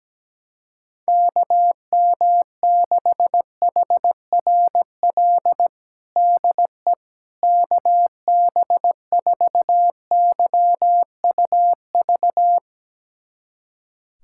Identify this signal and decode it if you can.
Morse code, heard as a keyed tone, about 17 wpm, 700 Hz. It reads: KM6HRL DE KB4YUV
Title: USA Callsign DE Callsign 17wpm